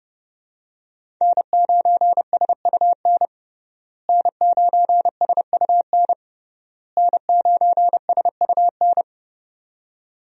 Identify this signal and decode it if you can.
Morse code, heard as a keyed tone, about 29 wpm, 700 Hz. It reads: N9SUN N9SUN N9SUN
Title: USA Call Signs - 5 Characters - Repeated 3x - 30wpm